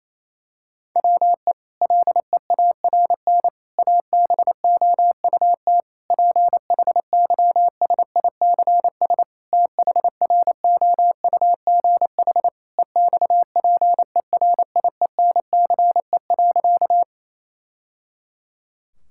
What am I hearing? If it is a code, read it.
Morse code, heard as a keyed tone, about 28 wpm, 700 Hz. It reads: WE LEARN ABOUT PHYSICS THROUGH EXPERIENCE.
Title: Sentences from Top 1000 Words - Review Part 5 - 28wpm